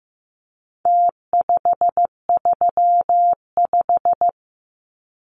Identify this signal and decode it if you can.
Morse code, heard as a keyed tone, about 15 wpm, 700 Hz. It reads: T535
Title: Numbers - Contest Style - 4-Digit 15wpm